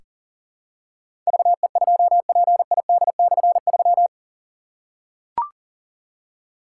Morse code, heard as a keyed tone, about 40 wpm, 700 Hz. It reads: VE2PID/3
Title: K1USN SST QSO Practice - 40wpm